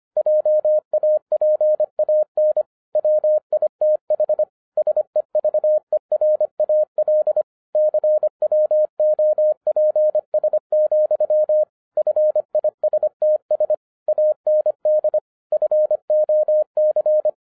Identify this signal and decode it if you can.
Morse code, heard as a keyed tone, about 24 wpm, 610 Hz. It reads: JAPAN WITH SEVERAL CWOPS, FISTS AND FOC